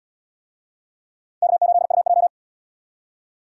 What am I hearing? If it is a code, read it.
Morse code, heard as a keyed tone, about 73 wpm, 700 Hz. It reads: K8UJ